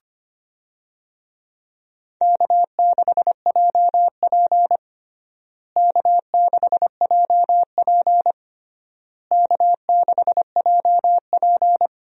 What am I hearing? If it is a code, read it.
Morse code, heard as a keyed tone, about 25 wpm, 700 Hz. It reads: K6JP K6JP K6JP
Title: USA Call Signs - 4 Characters - Repeated 3x - 25wpm